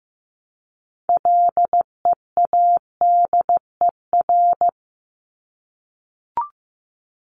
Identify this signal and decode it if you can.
Morse code, heard as a keyed tone, about 15 wpm, 700 Hz. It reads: LEADER
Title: Top 1501-1750 Words 15wpm